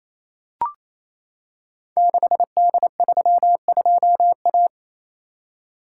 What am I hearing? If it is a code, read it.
Morse code, heard as a keyed tone, about 27 wpm, 700 Hz. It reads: 6D32A
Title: Code Groups - 5 Letter:Numbers - 28wpm